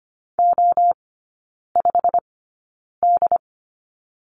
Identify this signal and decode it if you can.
Morse code, heard as a keyed tone, about 25 wpm, spaced as for 9 wpm, 690 Hz. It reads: O5D
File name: Lesson 15e - Number 5 - Three Characters 25wpm